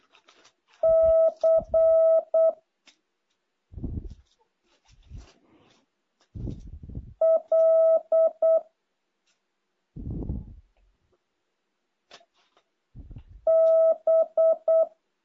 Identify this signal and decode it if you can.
Morse code, heard as a keyed tone, about 8 wpm, 650 Hz. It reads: C L B
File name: jh8cbh / 2023cw